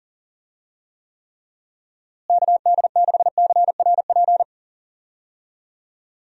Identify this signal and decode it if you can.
Morse code, heard as a keyed tone, about 40 wpm, 700 Hz. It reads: KD6CRP
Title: USA Call Signs - 6 Characters - Repeated 3x - 40wpm